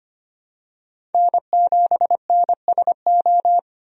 Morse code, heard as a keyed tone, about 25 wpm, 700 Hz. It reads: N7NSO
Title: USA Call Signs - 5 Characters - 25wpm